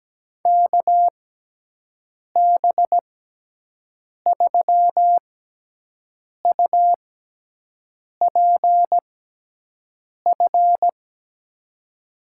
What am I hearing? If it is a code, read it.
Morse code, heard as a keyed tone, about 17 wpm, spaced as for 6 wpm, 700 Hz. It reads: KB3UPF
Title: USA Call Signs - 6 Characters - 6x Character Spacing - 17wpm